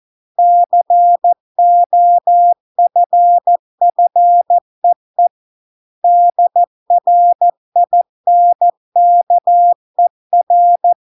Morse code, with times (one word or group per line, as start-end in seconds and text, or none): 0.4-5.3 COFFEE
6.0-10.9 DRINKER